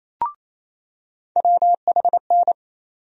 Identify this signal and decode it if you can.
Morse code, heard as a keyed tone, about 28 wpm, 700 Hz. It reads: WHN
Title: USA Call Sign Postfixes - 3 Letters 28wpm